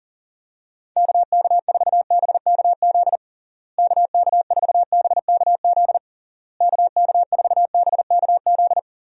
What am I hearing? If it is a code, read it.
Morse code, heard as a keyed tone, about 40 wpm, 700 Hz. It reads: KK4BKZ KK4BKZ KK4BKZ
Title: USA Call Signs - 6 Characters - Repeated 3x - 40wpm